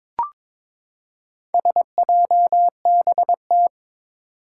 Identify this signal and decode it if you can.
Morse code, heard as a keyed tone, about 22 wpm, 700 Hz. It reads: SJBT
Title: Four Challenging Letter-Number Combinations 22wpm